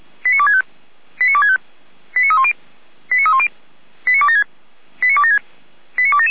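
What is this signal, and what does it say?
Five-tone selective call (ZVEI), heard as quick runs of five tones, tones of about 70 ms. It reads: 89265 89265 89310 89310 89276 89276 89277
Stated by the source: ZVEI five-tone call received by SDR, logged on audio.f85360.feuerwehr-orlamuende.org.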